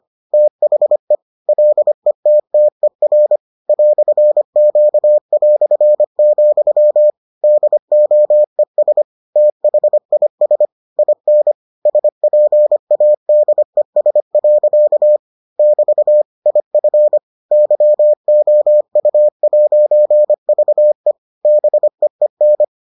Morse code, heard as a keyed tone, about 25 wpm, 600 Hz. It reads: THE LETTER "Q", DOES THIS IN SPADES. <BT> IF YOU'VE BEEN